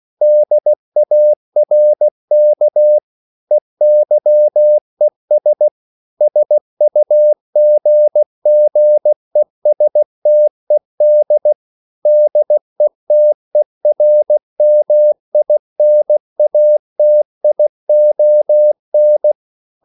Morse code, heard as a keyed tone, about 16 wpm, 600 Hz. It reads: DARK EYES SUGGESTED DETERMINATION